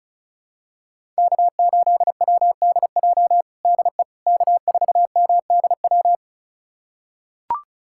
Morse code, heard as a keyed tone, about 34 wpm, 700 Hz. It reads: K8WDJ DE K4MDW
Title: USA Callsign DE Callsign - 5 Characters - 35wpm